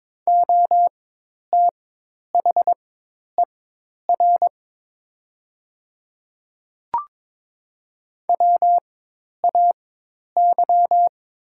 Morse code, heard as a keyed tone, about 22 wpm, spaced as for 10 wpm, 700 Hz. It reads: OTHER WAY
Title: 4x Character Spacing - Top 100 Words - 22wpm